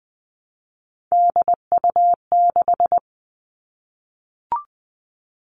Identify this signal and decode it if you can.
Morse code, heard as a keyed tone, about 20 wpm, 700 Hz. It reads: DU6
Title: All Cut Numbers - 3-Digit 20wpm